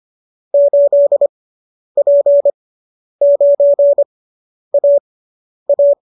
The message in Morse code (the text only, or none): 8P9AA